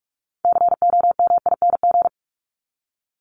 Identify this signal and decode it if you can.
Morse code, heard as a keyed tone, about 44 wpm, 700 Hz. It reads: COMING
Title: ING Suffix - Top 100 Words with Base Words 45wpm